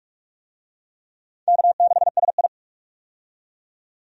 Morse code, heard as a keyed tone, about 41 wpm, 700 Hz. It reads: K6SI
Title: USA Call Signs - 4 Characters - 45wpm